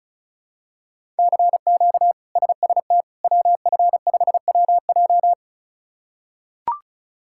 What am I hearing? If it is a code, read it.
Morse code, heard as a keyed tone, about 36 wpm, 700 Hz. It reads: CQ SST WF5WJ
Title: K1USN SST QSO Practice - 35wpm